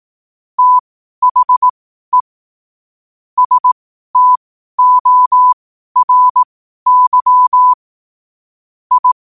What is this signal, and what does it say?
Morse code, heard as a keyed tone, about 18 wpm, spaced as for 12 wpm, 1000 Hz. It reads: THE STORY I
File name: qotdfiles / 12wpm